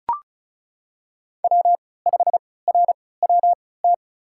Morse code, heard as a keyed tone, about 34 wpm, spaced as for 20 wpm, 700 Hz. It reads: W5RWT
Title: USA Call Signs - 5 Characters - 3x Character Spacing - 35wpm